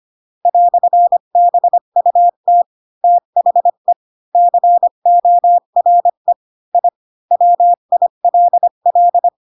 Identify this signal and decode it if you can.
Morse code, heard as a keyed tone, about 25 wpm, 700 Hz. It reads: "BUT THE CORE I WILL
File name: Pinocchio / Pinocchio ch7 20 25 ik1hge